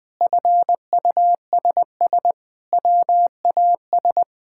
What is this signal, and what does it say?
Morse code, heard as a keyed tone, about 21 wpm, 700 Hz. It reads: FUSS WAS
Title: Sentences with 8-Word Phrases - 20wpm